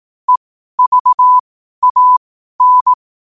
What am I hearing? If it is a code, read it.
Morse code, heard as a keyed tone, about 18 wpm, spaced as for 12 wpm, 1000 Hz. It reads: EVAN
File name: qotdfiles / 12wpm